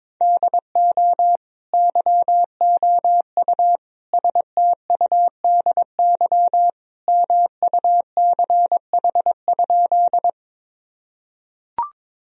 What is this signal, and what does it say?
Morse code, heard as a keyed tone, about 22 wpm, 700 Hz. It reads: DO YOU STUDY MUCH?
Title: Sentences from Top 1000 Words - Review Part 1 - 22wpm